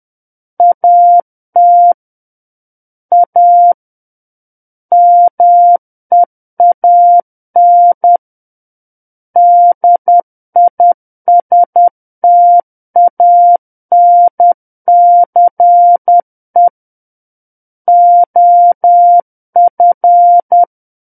Morse code, heard as a keyed tone, about 10 wpm, 700 Hz. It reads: AT A MEAN DISTANCE OF
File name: Book One / 01-Chapter 1 - 10 WPM